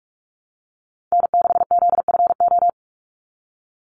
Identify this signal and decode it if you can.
Morse code, heard as a keyed tone, about 44 wpm, 700 Hz. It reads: N6ZFO
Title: Call Signs - 45wpm